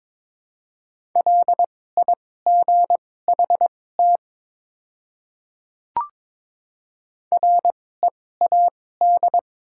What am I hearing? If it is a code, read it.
Morse code, heard as a keyed tone, about 22 wpm, spaced as for 16 wpm, 700 Hz. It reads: LIGHT READ
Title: 2x Character Spacing - Top 300 Words - 22wpm